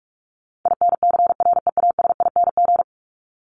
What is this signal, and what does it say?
Morse code, heard as a keyed tone, about 44 wpm, 700 Hz. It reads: INCREASING
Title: ING Suffix - Top 100 Words with Base Words 45wpm